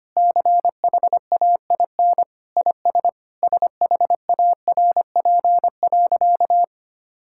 Morse code, heard as a keyed tone, about 25 wpm, 700 Hz. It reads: CHAIN IS SHARP.
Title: Sentences from Top 1600 Words 25wpm